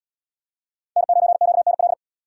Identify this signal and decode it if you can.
Morse code, heard as a keyed tone, about 70 wpm, 700 Hz. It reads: N9QNL